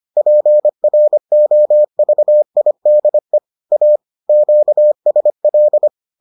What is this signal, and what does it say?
Morse code, heard as a keyed tone, about 25 wpm, 600 Hz. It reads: PROVIDE A QSL